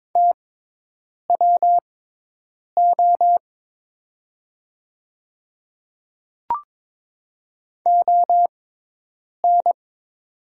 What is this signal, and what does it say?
Morse code, heard as a keyed tone, about 22 wpm, spaced as for 8 wpm, 700 Hz. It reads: TWO ON
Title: 6x Character Spacing - Top 100 Words - 22wpm